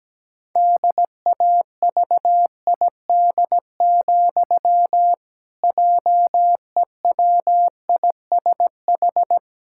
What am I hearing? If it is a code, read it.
Morse code, heard as a keyed tone, about 17 wpm, 700 Hz. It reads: DAVID, JEWISH
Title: Pangram Challenge v2 - 17wpm